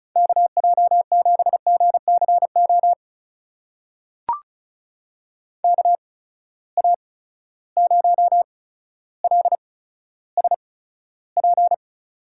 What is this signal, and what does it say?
Morse code, heard as a keyed tone, about 36 wpm, spaced as for 15 wpm, 700 Hz. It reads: KJ7GCO KA0LSP
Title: USA Call Signs - 6 Characters - 8x Character Spacing - 35wpm